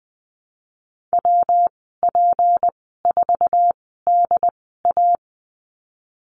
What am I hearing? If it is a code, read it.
Morse code, heard as a keyed tone, about 20 wpm, spaced as for 15 wpm, 700 Hz. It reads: WP4DA
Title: 167 Lesson 27g - Number 7 - Call Signs 20wpm